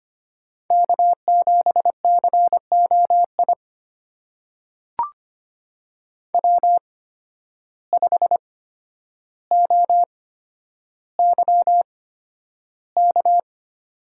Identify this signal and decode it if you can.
Morse code, heard as a keyed tone, about 25 wpm, spaced as for 11 wpm, 700 Hz. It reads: K7COI W5OYK
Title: USA Call Signs - 5 Characters - 8x Character Spacing - 25wpm